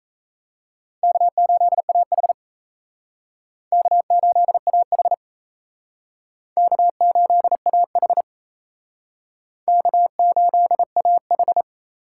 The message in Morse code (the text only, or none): K8AH K8AH K8AH K8AH